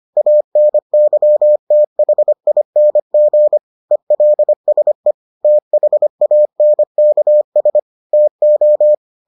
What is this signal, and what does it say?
Morse code, heard as a keyed tone, about 25 wpm, 600 Hz. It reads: ANYTHING ELSE THANKS TO